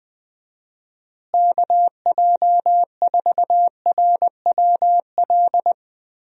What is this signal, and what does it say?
Morse code, heard as a keyed tone, about 20 wpm, 700 Hz. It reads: KJ4RWL
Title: USA Call Signs - 6 Characters - Repeated 3x - 20wpm